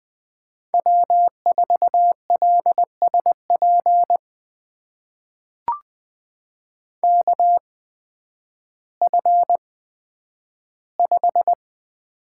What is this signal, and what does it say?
Morse code, heard as a keyed tone, about 20 wpm, spaced as for 11 wpm, 700 Hz. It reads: W4LSP KF5
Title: USA Call Signs - 5 Characters - 8x Character Spacing - 20wpm